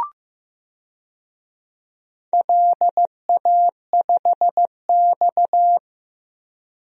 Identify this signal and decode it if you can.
Morse code, heard as a keyed tone, about 16 wpm, 700 Hz. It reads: LA5X
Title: DXCC-100 Call Signs - 4 Characters - Repeated 3x - 15wpm